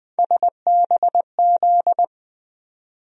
Morse code, heard as a keyed tone, about 20 wpm, 700 Hz. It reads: SBZ
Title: USA Call Sign Postfixes - 3 Letters 20wpm